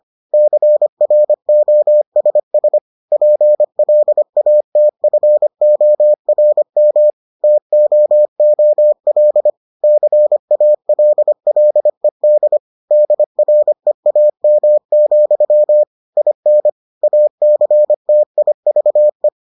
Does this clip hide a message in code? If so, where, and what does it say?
Yes: Morse code from 0.3 to 19.3 s, heard as a keyed tone, about 25 wpm, 600 Hz. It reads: CROSS PLATFORM TOOL CALLED DREAM, IN ACTIVE